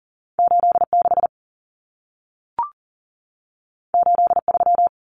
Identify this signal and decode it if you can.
Morse code, heard as a keyed tone, about 40 wpm, 700 Hz. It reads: 86 83
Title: Numbers - 2-Digit 40wpm